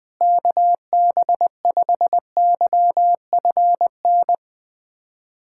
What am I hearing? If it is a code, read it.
Morse code, heard as a keyed tone, about 20 wpm, 690 Hz. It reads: KB5YFN
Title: SOTA QSO Practice 20wpm